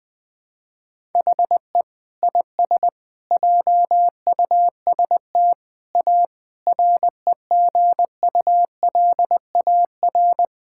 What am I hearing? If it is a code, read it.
Morse code, heard as a keyed tone, about 20 wpm, 700 Hz. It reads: HE IS JUST A REGULAR
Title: Sentences from Top 1000 Words - Review Part 6 - 20wpm